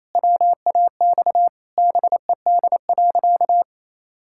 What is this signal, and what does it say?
Morse code, heard as a keyed tone, about 27 wpm, 700 Hz. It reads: WAX BED.
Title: Pangram Challenge v2 - 28wpm